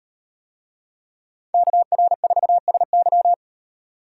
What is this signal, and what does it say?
Morse code, heard as a keyed tone, about 38 wpm, 700 Hz. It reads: KR4SY